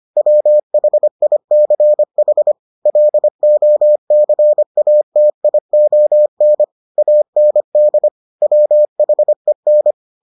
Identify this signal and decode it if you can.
Morse code, heard as a keyed tone, about 25 wpm, 600 Hz. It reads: WHICH LOCATION AND WHEN